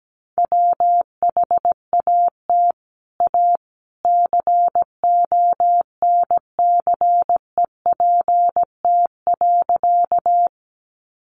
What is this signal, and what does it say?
Morse code, heard as a keyed tone, about 17 wpm, 700 Hz. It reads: WHAT A CONCEPT.
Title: Sentences from Top 1100 Words 17wpm